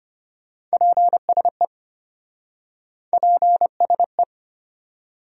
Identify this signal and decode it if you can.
Morse code, heard as a keyed tone, about 34 wpm, 700 Hz. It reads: PSE PSE